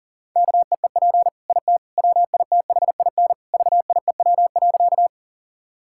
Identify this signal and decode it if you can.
Morse code, heard as a keyed tone, about 37 wpm, 700 Hz. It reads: KEEP IT WITHIN VIEW.
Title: Sentences from Top 1000 Words - Review Part 6 - 40wpm